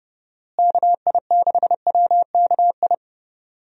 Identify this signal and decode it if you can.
Morse code, heard as a keyed tone, about 30 wpm, 700 Hz. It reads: KI6WKI
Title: USA Call Signs - 6 Characters - Repeated 3x - 30wpm